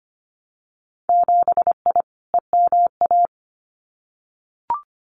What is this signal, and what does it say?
Morse code, heard as a keyed tone, about 25 wpm, 700 Hz. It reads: 7I EMA
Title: Winter Field Day Exchanges Encore 25wpm